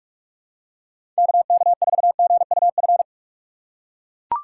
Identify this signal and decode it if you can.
Morse code, heard as a keyed tone, about 44 wpm, 700 Hz. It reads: KK4GUF
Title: USA Call Signs - 6 Characters - 3x Character Spacing - 45wpm